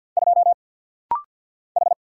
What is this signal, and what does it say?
Morse code, heard as a keyed tone, about 50 wpm, 700 Hz. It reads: 2 H
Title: Single Letter-Number - Rapid-Fire 50wpm